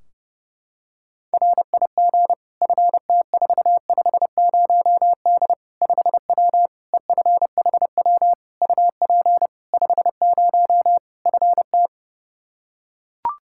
Morse code, heard as a keyed tone, about 30 wpm, 700 Hz. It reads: RIG FT450D 5W EFHW UP 50 FT
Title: Practice QSO - Exchange 2 - Info - 30wpm